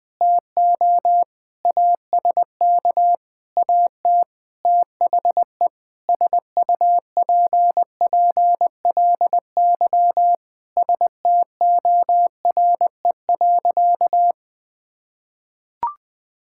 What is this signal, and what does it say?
Morse code, heard as a keyed tone, about 20 wpm, 700 Hz. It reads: TO ASK AT THE SUPPLY STORE.
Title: Sentences from Top 1000 Words - Review Part 5 - 20wpm